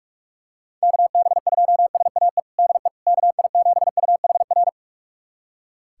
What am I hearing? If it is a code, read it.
Morse code, heard as a keyed tone, about 46 wpm, 700 Hz. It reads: KB2SAE DE KI7UHR
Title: USA Callsign DE Callsign - 6 Characters - 45wpm